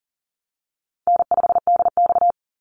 Morse code, heard as a keyed tone, about 40 wpm, 700 Hz. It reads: N5DX
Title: Easy Call Signs Repeated 3x - 40wpm